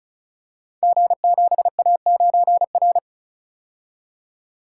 Morse code, heard as a keyed tone, about 35 wpm, 700 Hz. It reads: G7A9R
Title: Code Groups - 5 Letter:Numbers - 35wpm